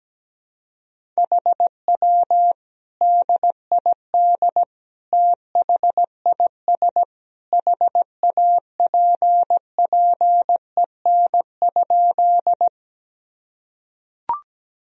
Morse code, heard as a keyed tone, about 17 wpm, 700 Hz. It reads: HW DID THIS HAPPEN?
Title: Sentences from Top 400 Words and Abbreviations 17wpm